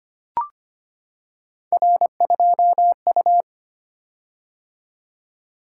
Morse code, heard as a keyed tone, about 25 wpm, 700 Hz. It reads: R2U
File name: Three Letter-Number Combinations - 25wpm